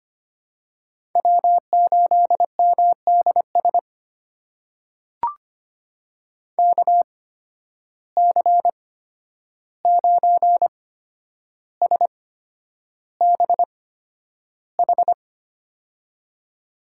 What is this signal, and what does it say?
Morse code, heard as a keyed tone, about 25 wpm, spaced as for 10 wpm, 700 Hz. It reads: W8MDS KC9SBH
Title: USA Call Signs - 8x Character Spacing - 25wpm